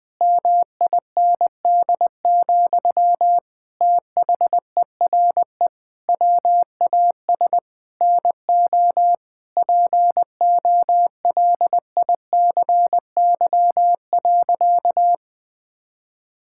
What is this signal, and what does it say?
Morse code, heard as a keyed tone, about 21 wpm, 700 Hz. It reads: MIND, THERE WAS NO POLICY.
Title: Sentences from Top 1000 Words - Review Part 5 - 20wpm